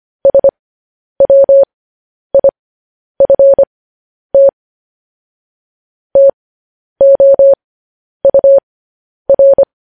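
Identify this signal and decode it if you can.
Morse code, heard as a keyed tone, about 25 wpm, spaced as for 10 wpm, 550 Hz. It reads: SWIFT TOUR
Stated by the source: Morse code transcription of news headlines sent at 10 wpm.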